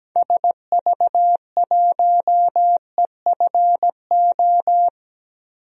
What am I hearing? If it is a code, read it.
Morse code, heard as a keyed tone, about 17 wpm, 700 Hz. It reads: SV1EFO
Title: DXCC-100 Call Signs - 6 Characters - Repeated 3x - 17wpm